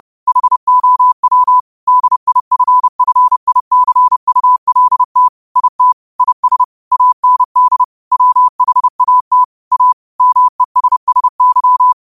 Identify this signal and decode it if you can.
Morse code, heard as a keyed tone, about 30 wpm, 1000 Hz. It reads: HOW DIFFICULT IT IS AND WHAT A MESSY